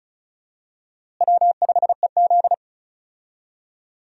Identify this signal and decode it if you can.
Morse code, heard as a keyed tone, about 36 wpm, 700 Hz. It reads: W5EZ
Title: USA Call Signs - 4 Characters - Repeated 3x - 35wpm